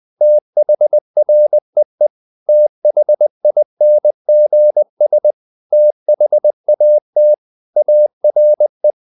Morse code, heard as a keyed tone, about 20 wpm, 600 Hz. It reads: THREE THINGS THAT ARE
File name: the old man and the sea / the old man and the sea part 1 20wpm